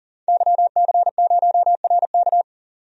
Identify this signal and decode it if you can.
Morse code, heard as a keyed tone, about 38 wpm, 700 Hz. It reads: YC0RK
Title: DXCC-100 Call Signs - 5 Characters - 40wpm